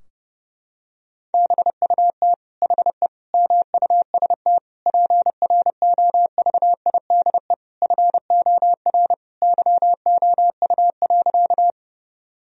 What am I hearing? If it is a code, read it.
Morse code, heard as a keyed tone, about 30 wpm, 700 Hz. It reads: BUT HE MUST PROVIDE FOR YOU.
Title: Sentences from Top 1000 Words - Review Part 3 - 30wpm